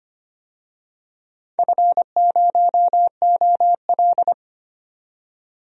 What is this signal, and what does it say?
Morse code, heard as a keyed tone, about 25 wpm, 700 Hz. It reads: F0OL
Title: DXCC-100 Call Signs - 4 Characters - 25wpm